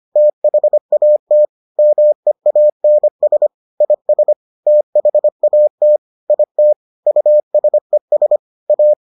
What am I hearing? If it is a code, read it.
Morse code, heard as a keyed tone, about 25 wpm, 600 Hz. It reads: THAT MEANS IS THAT IT USES A